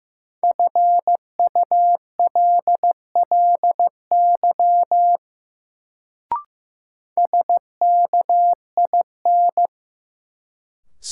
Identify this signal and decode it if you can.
Morse code, heard as a keyed tone, about 15 wpm, 700 Hz. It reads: FULLY SKIN
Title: Sentences from Top 1300 Words - New Word Drill - 15wpm